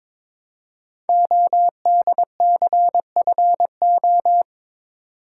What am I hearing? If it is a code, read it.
Morse code, heard as a keyed tone, about 22 wpm, 700 Hz. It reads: ODCFO
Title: Code Groups - 5 Letters - 22wpm